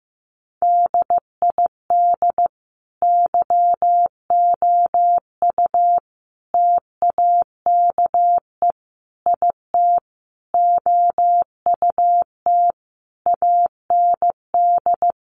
Morse code, heard as a keyed tone, about 15 wpm, 700 Hz. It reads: DID YOU TAKE IT OUT AND
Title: Sentences from Top 1100 Words 15wpm